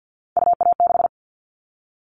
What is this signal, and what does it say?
Morse code, heard as a keyed tone, about 50 wpm, 700 Hz. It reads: UA6
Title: Callsign Prefixes for Top 50 DXCC - 50wpm